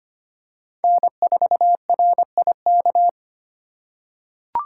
Morse code, heard as a keyed tone, about 25 wpm, 700 Hz. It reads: N4RIK
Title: USA Call Signs - 5 Characters - 2x Character Spacing - 25wpm